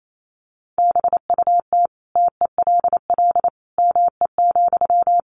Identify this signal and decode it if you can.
Morse code, heard as a keyed tone, about 28 wpm, 700 Hz. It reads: BUT TELL ME,
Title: Sentences from Top 1500 Words 28wpm